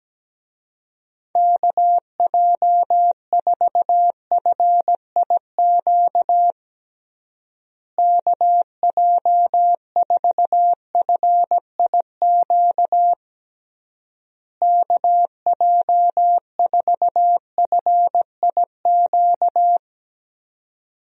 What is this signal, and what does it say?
Morse code, heard as a keyed tone, about 17 wpm, 700 Hz. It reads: KJ4FIQ KJ4FIQ KJ4FIQ
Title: USA Call Signs - 6 Characters - Repeated 3x - 17wpm